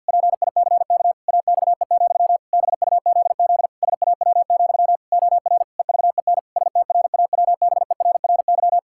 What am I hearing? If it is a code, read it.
Morse code, heard as a keyed tone, about 49 wpm, 700 Hz. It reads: PICK AXE, BUZZ SAW, OR EVEN STRAWBERRY
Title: Pangram Challenge v2 - 50wpm